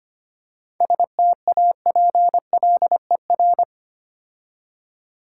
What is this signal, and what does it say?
Morse code, heard as a keyed tone, about 30 wpm, 690 Hz. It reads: STAPLER
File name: Easy Words - Speed-Racing with Hints - 25wpm